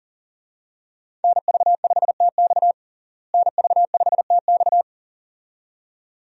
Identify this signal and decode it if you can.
Morse code, heard as a keyed tone, about 40 wpm, 700 Hz. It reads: NV5TX NV5TX
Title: USA Call Signs - 5 Characters - Repeated 2x - 40wpm